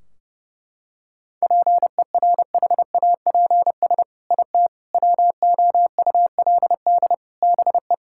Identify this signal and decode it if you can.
Morse code, heard as a keyed tone, about 32 wpm, 700 Hz. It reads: PERHAPS IT WOULD BE
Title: Sentences from Top 1600 Words 30wpm